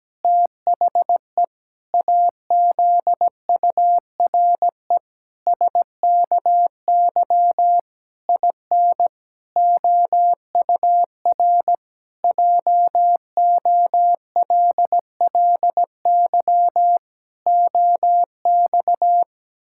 Morse code, heard as a keyed tone, about 17 wpm, 700 Hz. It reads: THE AZURE SKY IN OUR JOLLY OX
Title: Pangram Challenge v2 - 17wpm